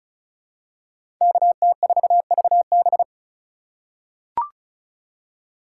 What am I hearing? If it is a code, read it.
Morse code, heard as a keyed tone, about 36 wpm, 700 Hz. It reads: KT4VB
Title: USA Call Signs - 3x Character Spacing - 35wpm